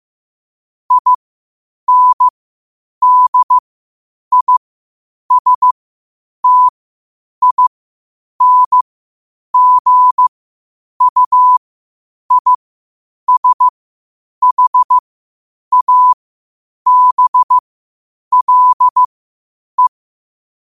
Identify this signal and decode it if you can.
Morse code, heard as a keyed tone, about 15 wpm, spaced as for 8 wpm, 1000 Hz. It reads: INDISTINGUISHABLE
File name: qotdfiles / 8wpm